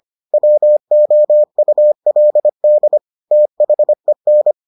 Morse code, heard as a keyed tone, about 25 wpm, 600 Hz. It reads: WOULD THEN